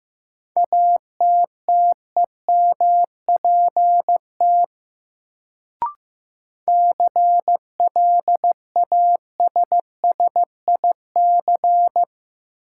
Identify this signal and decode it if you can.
Morse code, heard as a keyed tone, about 15 wpm, 700 Hz. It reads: ATTEMPT CLASSIC
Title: Sentences from Top 1000 Words - New Word Drill - 15wpm